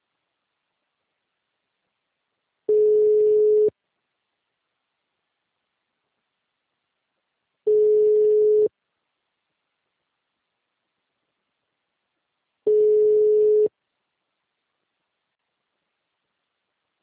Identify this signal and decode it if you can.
Ringback tone (Europe, 425 Hz).